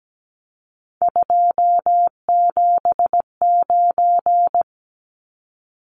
Morse code, heard as a keyed tone, about 17 wpm, 700 Hz. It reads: 279
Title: Numbers - 3-Digit 17wpm